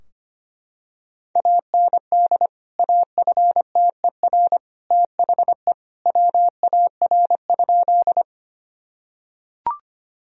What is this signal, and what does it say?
Morse code, heard as a keyed tone, about 25 wpm, 700 Hz. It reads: AND AFTER THE WAR?
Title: Sentences from Top 1000 Words - Review Part 2 - 25wpm